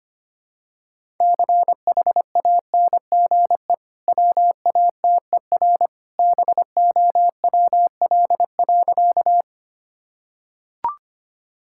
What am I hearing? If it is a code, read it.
Morse code, heard as a keyed tone, about 25 wpm, 700 Hz. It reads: CHANGE WATER BOWL.
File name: Easy Sentences with Hints - 25wpm